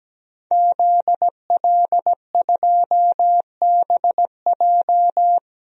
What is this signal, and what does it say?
Morse code, heard as a keyed tone, about 17 wpm, 700 Hz. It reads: ZL2BJ
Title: DXCC-100 Call Signs - 5 Characters - 17wpm